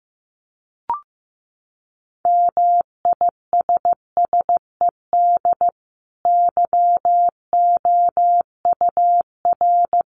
Morse code, heard as a keyed tone, about 14 wpm, 700 Hz. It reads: MISSED YOUR
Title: Common QSO Phrases 15wpm